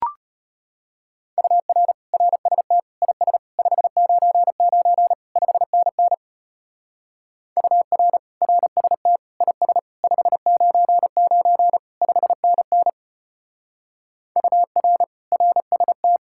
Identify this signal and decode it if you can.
Morse code, heard as a keyed tone, about 31 wpm, 700 Hz. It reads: UR RST IS 599 5NN UR RST IS 599 5NN UR RST
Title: Speed-Racing Common QSO Phrases - 25wpm